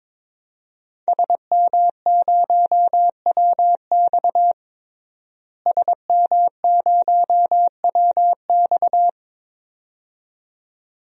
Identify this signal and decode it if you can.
Morse code, heard as a keyed tone, about 22 wpm, 700 Hz. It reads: SM0WX SM0WX
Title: DXCC-100 Call Signs - 5 Characters - Repeated 2x - 22wpm